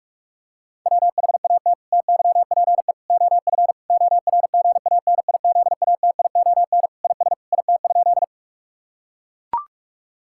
Morse code, heard as a keyed tone, about 41 wpm, 700 Hz. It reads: WHAT TYPE OF ORGANIZATION IS IT?
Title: Sentences from Top 1000 Words - Review Part 5 - 45wpm